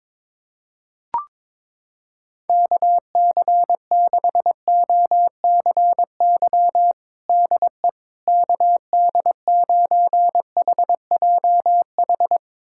Morse code, heard as a keyed tone, about 22 wpm, 700 Hz. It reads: KC6OCY DE KD9HJH
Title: USA Callsign DE Callsign - 6 Characters - 22wpm